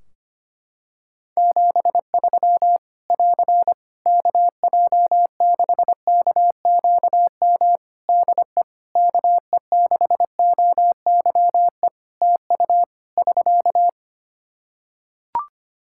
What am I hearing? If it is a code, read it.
Morse code, heard as a keyed tone, about 25 wpm, 700 Hz. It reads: 73 <AR> KJ6KQM DE KE6OYE TU <SK>
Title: Practice QSO - Exchange 4 - Ending - 25wpm